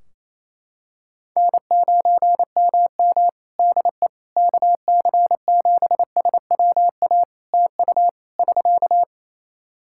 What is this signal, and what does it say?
Morse code, heard as a keyed tone, about 28 wpm, 700 Hz. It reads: N9MM DE KC7SWA TU <SK>
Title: Practice QSO - Exchange 4 - Ending - 28wpm